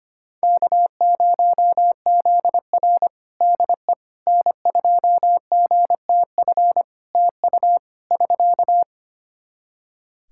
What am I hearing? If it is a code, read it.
Morse code, heard as a keyed tone, about 25 wpm, 700 Hz. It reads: K0ZR DE N2GTF TU <SK>
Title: Practice QSO - Exchange 4 - Ending - 25wpm